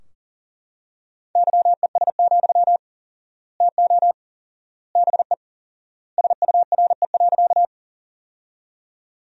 Morse code, heard as a keyed tone, about 40 wpm, 700 Hz. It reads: YES, TO BE SURE.
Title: 4x Word-Spacing - Sentences from Top 500 Words - 40wpm